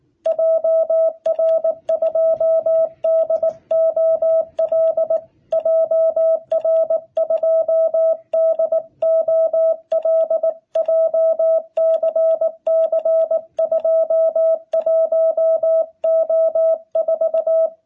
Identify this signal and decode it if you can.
Morse code, heard as a keyed tone, about 19 wpm, 660 Hz. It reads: JR2DOL JR2DOL JCC 21O4
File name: jh8cbh / 2021cw